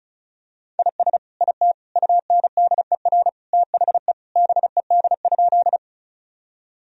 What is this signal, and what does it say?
Morse code, heard as a keyed tone, about 36 wpm, 690 Hz. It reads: IS IT UNDER THE BED?
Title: Sentences from Top 1000 Words - Review Part 1 - 35wpm